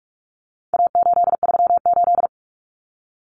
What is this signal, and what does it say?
Morse code, heard as a keyed tone, about 44 wpm, 700 Hz. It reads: A838
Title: Common Cut Numbers - 4-Digit 45wpm